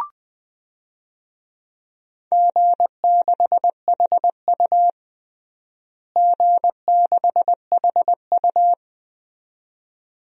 Morse code, heard as a keyed tone, about 20 wpm, 700 Hz. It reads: G6HU G6HU
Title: DXCC-100 Call Signs - 4 Characters - Repeated 2x - 20wpm